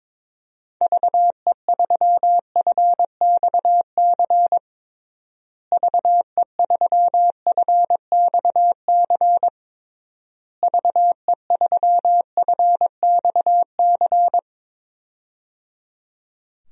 Morse code, heard as a keyed tone, about 22 wpm, 700 Hz. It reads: VE3FXC VE3FXC VE3FXC
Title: DXCC-100 Call Signs - 6 Characters - Repeated 3x - 22wpm